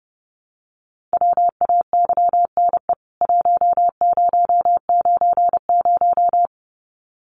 Morse code, heard as a keyed tone, about 30 wpm, 700 Hz. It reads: WAYNE 1090
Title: CWT Exchange 30wpm